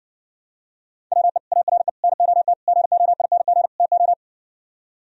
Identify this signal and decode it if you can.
Morse code, heard as a keyed tone, about 66 wpm, 700 Hz. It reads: WE ARE NOT GOING TO